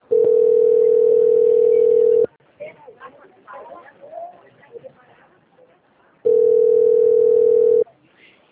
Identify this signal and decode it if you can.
Ringback tone (North America), 440 + 480 Hz, with speech.
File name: out- - - - - .66792